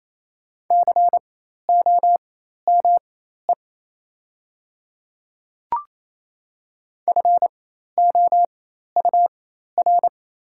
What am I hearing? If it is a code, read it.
Morse code, heard as a keyed tone, about 27 wpm, spaced as for 13 wpm, 700 Hz. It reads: COME FOUR
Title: 4x Character Spacing - Top 300 Words - 28wpm